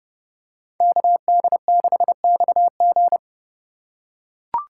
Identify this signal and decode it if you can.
Morse code, heard as a keyed tone, about 28 wpm, 700 Hz. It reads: KD6XG